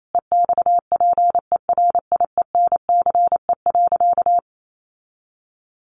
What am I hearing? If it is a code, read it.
Morse code, heard as a keyed tone, about 28 wpm, 700 Hz. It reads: EXPERIENCE.
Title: Sentences from Top 700 Words 28wpm